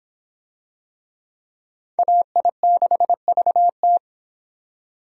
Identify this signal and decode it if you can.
Morse code, heard as a keyed tone, about 25 wpm, 700 Hz. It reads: AI6VT